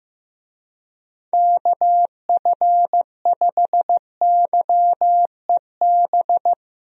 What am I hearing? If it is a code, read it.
Morse code, heard as a keyed tone, about 15 wpm, 700 Hz. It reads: KF5YEB
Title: USA Call Signs - 6 Characters - 15wpm